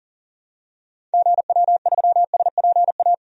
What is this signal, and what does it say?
Morse code, heard as a keyed tone, about 40 wpm, 700 Hz. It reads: GW3SPA
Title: DXCC-100 Call Signs - 6 Characters - 40wpm